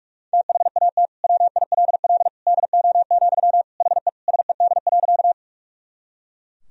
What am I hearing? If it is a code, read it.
Morse code, heard as a keyed tone, about 44 wpm, 700 Hz. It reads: THAT WILL DO, HE SED.
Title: Sentences from Top 100 Words and Abbreviations 45wpm